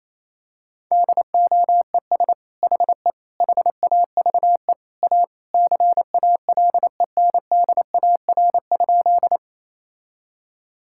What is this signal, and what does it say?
Morse code, heard as a keyed tone, about 28 wpm, 700 Hz. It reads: DOES HE HAVE A CALENDAR?
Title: Sentences from Top 1600 Words 28wpm